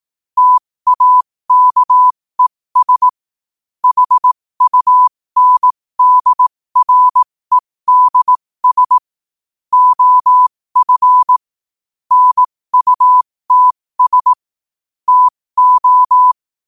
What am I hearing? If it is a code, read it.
Morse code, heard as a keyed tone, about 18 wpm, spaced as for 15 wpm, 1000 Hz. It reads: TAKES HUNDREDS OF NUTS TO
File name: qotdfiles / 15wpm